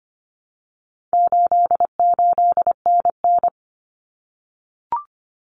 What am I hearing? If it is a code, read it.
Morse code, heard as a keyed tone, about 25 wpm, 700 Hz. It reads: 88NN
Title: Common Cut Numbers - 4-Digit 25wpm